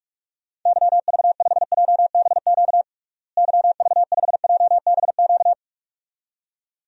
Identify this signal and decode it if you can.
Morse code, heard as a keyed tone, about 46 wpm, 700 Hz. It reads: YV5JBQ YV5JBQ
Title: DXCC-100 Call Signs - 6 Characters - Repeated 2x - 45wpm